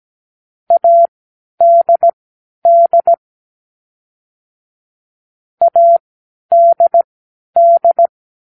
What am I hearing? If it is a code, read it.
Morse code, heard as a keyed tone, about 17 wpm, spaced as for 10 wpm, 680 Hz. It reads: ADD ADD
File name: k7qo / Practicewords